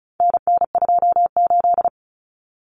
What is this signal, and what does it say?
Morse code, heard as a keyed tone, about 34 wpm, 700 Hz. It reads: NN28
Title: Common Cut Numbers - 4-Digit 35wpm